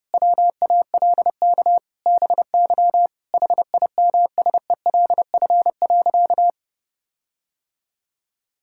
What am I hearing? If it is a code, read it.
Morse code, heard as a keyed tone, about 32 wpm, 700 Hz. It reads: WALK BY HIMSELF.
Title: Sentences from Top 1600 Words 30wpm